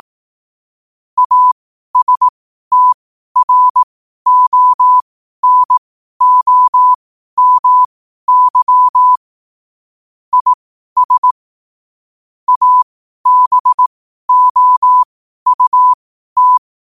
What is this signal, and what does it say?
Morse code, heard as a keyed tone, about 18 wpm, spaced as for 12 wpm, 1000 Hz. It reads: ASTRONOMY IS ABOUT
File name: qotdfiles / 12wpm